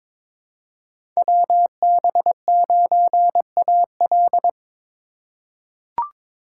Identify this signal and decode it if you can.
Morse code, heard as a keyed tone, about 22 wpm, 700 Hz. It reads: WB9AL
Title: USA Call Signs - 5 Characters - 8x Character Spacing - 22wpm